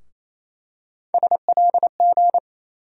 Morse code, heard as a keyed tone, about 28 wpm, 700 Hz. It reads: SLG
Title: USA Call Sign Postfixes - 3 Letters 28wpm